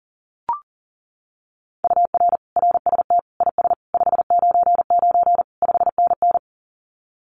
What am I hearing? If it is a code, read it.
Morse code, heard as a keyed tone, about 40 wpm, 700 Hz. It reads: UR RST IS 599 5NN
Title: Common QSO Phrases Encore 40wpm